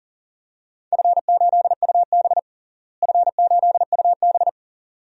Morse code, heard as a keyed tone, about 40 wpm, 700 Hz. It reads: F8UB F8UB
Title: DXCC-100 Call Signs - 4 Characters - Repeated 2x - 40wpm